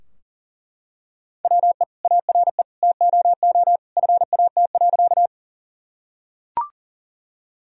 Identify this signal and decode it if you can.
Morse code, heard as a keyed tone, about 40 wpm, 700 Hz. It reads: WE ARE TOO FAT.
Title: Sentences from Top 1000 Words - Review Part 4 - 40wpm